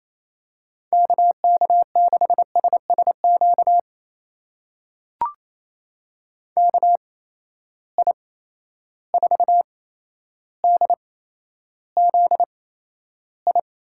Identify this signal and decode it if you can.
Morse code, heard as a keyed tone, about 28 wpm, spaced as for 12 wpm, 700 Hz. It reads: KK6SSQ KI4DZI
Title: USA Call Signs - 6 Characters - 8x Character Spacing - 28wpm